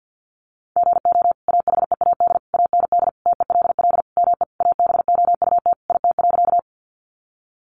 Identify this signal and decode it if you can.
Morse code, heard as a keyed tone, about 49 wpm, 700 Hz. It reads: GO AHEAD AND TELL ME ABOUT IT.
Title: Sentences from Top 1100 Words 50wpm